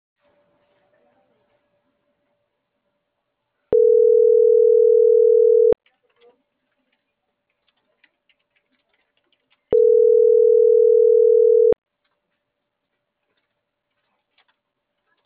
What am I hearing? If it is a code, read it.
Ringback tone (North America), 440 + 480 Hz.
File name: CENTRAL MAYAB / out michelle.martinez